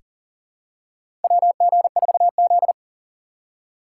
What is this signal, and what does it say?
Morse code, heard as a keyed tone, about 40 wpm, 700 Hz. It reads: WG4Z
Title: USA Call Signs - 4 Characters - 3x Character Spacing - 40wpm